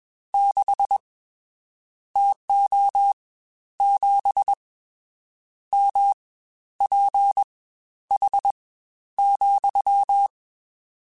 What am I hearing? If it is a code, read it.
Morse code, heard as a keyed tone, about 21 wpm, spaced as for 11 wpm, 800 Hz. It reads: 6 TO 7 MPH,
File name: Spaced Characters / Spaced Characters-mp3s / Weather 4 24 sp @20